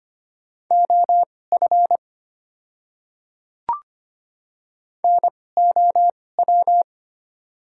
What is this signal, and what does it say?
Morse code, heard as a keyed tone, about 25 wpm, spaced as for 18 wpm, 690 Hz. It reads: OF NOW
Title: 2x Character Spacing - Top 100 Words - 25wpm